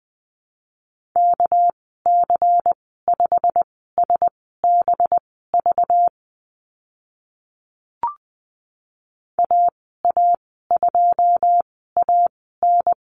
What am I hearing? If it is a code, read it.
Morse code, heard as a keyed tone, about 20 wpm, spaced as for 14 wpm, 700 Hz. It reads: KC5SBV AA2AN
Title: 231 Lesson 37g - Letter x - Call Signs 20wpm